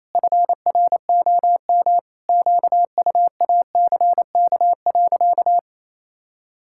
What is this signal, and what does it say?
Morse code, heard as a keyed tone, about 28 wpm, 700 Hz. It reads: FROM QUACK.
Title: Pangram Challenge v2 - 28wpm